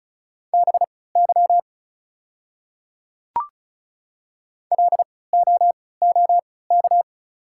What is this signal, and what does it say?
Morse code, heard as a keyed tone, about 35 wpm, spaced as for 20 wpm, 700 Hz. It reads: BY LOOK
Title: 3x Character Spacing - Top 100 Words - 35wpm